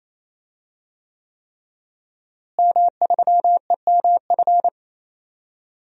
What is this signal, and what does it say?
Morse code, heard as a keyed tone, about 28 wpm, 700 Hz. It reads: M3EMF
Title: DXCC-100 Call Signs - 5 Characters - Repeated 3x - 28wpm